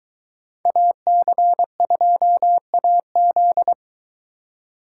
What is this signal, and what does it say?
Morse code, heard as a keyed tone, about 22 wpm, 700 Hz. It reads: AC2AZ